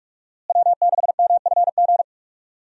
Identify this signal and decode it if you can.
Morse code, heard as a keyed tone, about 46 wpm, 700 Hz. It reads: W6MFG
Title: USA Call Signs - 5 Characters - Repeated 3x - 45wpm